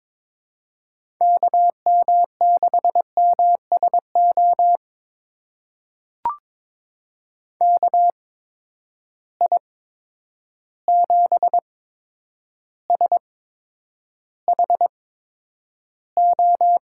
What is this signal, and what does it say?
Morse code, heard as a keyed tone, about 22 wpm, spaced as for 9 wpm, 700 Hz. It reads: KM6MSO KI7SHO
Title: USA Call Signs - 6 Characters - 8x Character Spacing - 22wpm